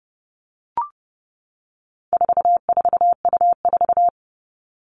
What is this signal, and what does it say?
Morse code, heard as a keyed tone, about 30 wpm, 700 Hz. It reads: 44U4
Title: All Cut Numbers - 4-Digit 30wpm